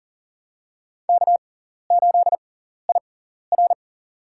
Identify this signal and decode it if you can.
Morse code, heard as a keyed tone, about 40 wpm, spaced as for 14 wpm, 700 Hz. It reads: K8IR
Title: USA Call Signs - 4 Characters - 6x Character Spacing - 40wpm